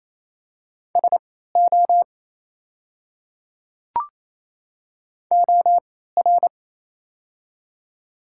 Morse code, heard as a keyed tone, about 27 wpm, spaced as for 16 wpm, 700 Hz. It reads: SO OR
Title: 3x Character Spacing - Top 100 Words - 28wpm